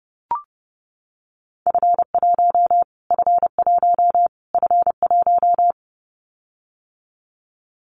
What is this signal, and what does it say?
Morse code, heard as a keyed tone, about 30 wpm, 700 Hz. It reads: F1 F1 F1
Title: Callsign Prefixes for Top 25 DXCC - Repeated 3x - 30wpm